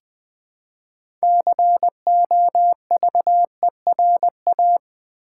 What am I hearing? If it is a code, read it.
Morse code, heard as a keyed tone, about 20 wpm, 700 Hz. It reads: COVERA
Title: Build A Word - 3 Letters At A Time - 8-Letter Words - 20wpm